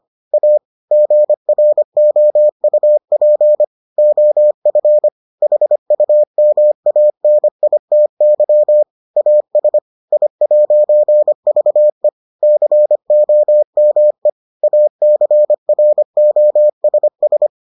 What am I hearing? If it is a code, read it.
Morse code, heard as a keyed tone, about 25 wpm, 600 Hz. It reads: A GROUP OF HUMANITY AS I'VE COME ACROSS